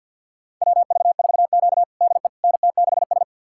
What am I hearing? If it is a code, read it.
Morse code, heard as a keyed tone, about 52 wpm, 700 Hz. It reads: WV4Q DE NT6S
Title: USA Callsign DE Callsign - 4 Characters - 50wpm